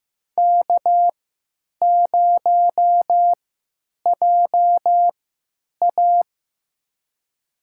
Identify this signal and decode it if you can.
Morse code, heard as a keyed tone, about 15 wpm, spaced as for 8 wpm, 700 Hz. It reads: K0JA
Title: USA Call Signs - 4 Characters - 3x Character Spacing - 15wpm